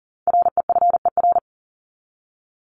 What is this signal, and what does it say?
Morse code, heard as a keyed tone, about 40 wpm, 700 Hz. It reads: REFER
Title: ED Suffix - Top 100 Words with Base Words 40wpm